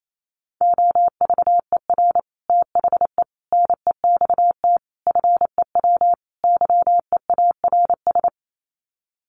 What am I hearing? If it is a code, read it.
Morse code, heard as a keyed tone, about 28 wpm, 690 Hz. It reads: OVER THE NEXT FEW YEARS
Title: Sets of 5 Words Encore 28wpm